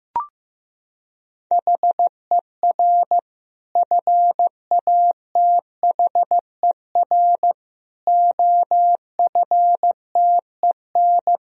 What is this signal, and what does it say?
Morse code, heard as a keyed tone, about 15 wpm, 700 Hz. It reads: HER FATHER OFTEN
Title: Sentences from Top 1000 Words - Review Part 1 - 15wpm